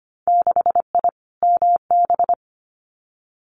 Morse code, heard as a keyed tone, about 25 wpm, 690 Hz. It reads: 6I MB
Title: Winter Field Day Exchanges Encore 25wpm